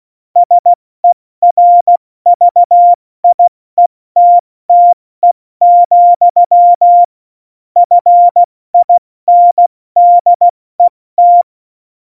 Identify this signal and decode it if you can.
Morse code, heard as a keyed tone, about 16 wpm, spaced as for 14 wpm, 700 Hz. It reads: SERVIETTE, FINDET
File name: Morsekurs / Witze / Witz 09. April 2019, 70 F80